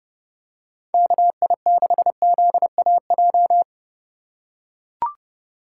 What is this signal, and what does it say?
Morse code, heard as a keyed tone, about 30 wpm, 700 Hz. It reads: KI6ZAJ
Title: USA Call Signs - 6 Characters - 3x Character Spacing - 30wpm